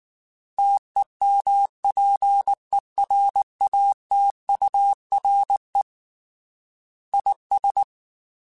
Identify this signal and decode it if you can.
Morse code, heard as a keyed tone, about 19 wpm, 800 Hz. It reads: TEMPERATURE IS
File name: Weather Sets / Weather Sets-mp3s / Weather 5 12 @18